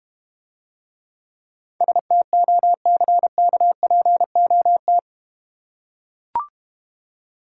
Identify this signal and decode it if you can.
Morse code, heard as a keyed tone, about 32 wpm, 700 Hz. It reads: STOCKPOT